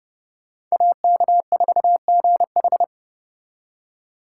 Morse code, heard as a keyed tone, about 37 wpm, 700 Hz. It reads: AK4GH
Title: Speed Racing - USA 2x2 Call Signs 30wpm